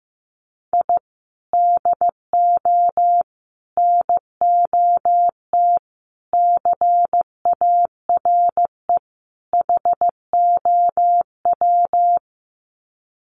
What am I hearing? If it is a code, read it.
Morse code, heard as a keyed tone, about 15 wpm, 700 Hz. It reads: I DO NOT CARE HOW
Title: Sets of 5 Words Encore2 15wpm